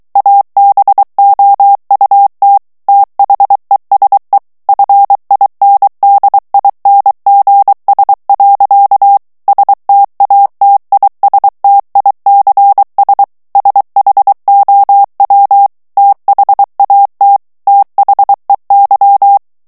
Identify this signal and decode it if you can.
Morse code, heard as a keyed tone, about 23 wpm, 800 Hz. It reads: ABOUT THESE FINDINGS. STATISTICS SHOW THAT THEY
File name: morse mp3 / ANGLAIS texte clair 1 vitesse 22 mots minute